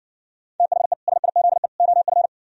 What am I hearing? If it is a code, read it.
Morse code, heard as a keyed tone, about 68 wpm, 700 Hz. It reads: THE SIZE OF